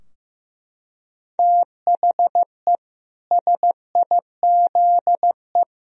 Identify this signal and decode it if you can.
Morse code, heard as a keyed tone, about 15 wpm, 700 Hz. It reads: THE SIZE
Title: Sentences from Top 1000 Words - Review Part 2 - 15wpm